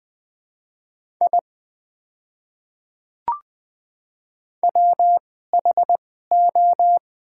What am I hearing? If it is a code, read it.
Morse code, heard as a keyed tone, about 21 wpm, spaced as for 15 wpm, 700 Hz. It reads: I WHO
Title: 2x Character Spacing - Top 200 Words - 20wpm